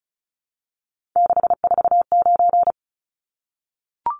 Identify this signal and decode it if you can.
Morse code, heard as a keyed tone, about 34 wpm, 700 Hz. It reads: 649
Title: Numbers - 3-Digit 35wpm